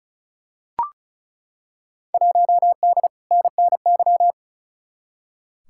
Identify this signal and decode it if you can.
Morse code, heard as a keyed tone, about 34 wpm, 700 Hz. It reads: 1D NNY
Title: Field Day Exchanges - 35wpm